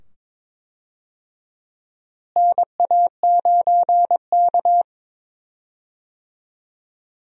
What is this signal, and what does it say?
Morse code, heard as a keyed tone, about 22 wpm, 700 Hz. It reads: NA9K
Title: USA Call Signs - 4 Characters - Repeated 2x - 22wpm